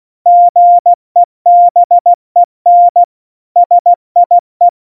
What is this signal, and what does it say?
Morse code, heard as a keyed tone, about 16 wpm, 700 Hz. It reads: GEBEN SIE
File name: Morsekurs / Witze / Witz 03. September 2019, 70 F80